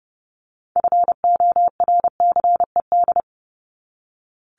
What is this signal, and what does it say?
Morse code, heard as a keyed tone, about 30 wpm, 700 Hz. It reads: FORCED
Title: ED Suffix - Top 100 Words with Base Words 30wpm